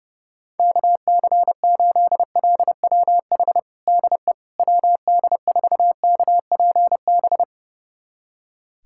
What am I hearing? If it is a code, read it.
Morse code, heard as a keyed tone, about 30 wpm, 700 Hz. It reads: KC8LWH DE WD4KPB
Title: USA Callsign DE Callsign 30wpm